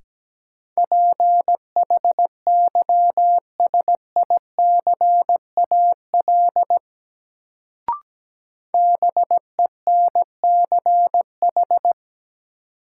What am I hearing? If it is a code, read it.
Morse code, heard as a keyed tone, about 17 wpm, 700 Hz. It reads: PHYSICAL BENCH
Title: Sentences from Top 1400 Words - New Word Drill Alt - 17wpm